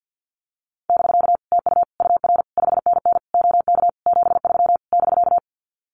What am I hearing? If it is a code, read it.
Morse code, heard as a keyed tone, about 50 wpm, 700 Hz. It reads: <BK> TU UR 5NN OK 73 <BK>
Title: Common QSO Phrases Encore 50wpm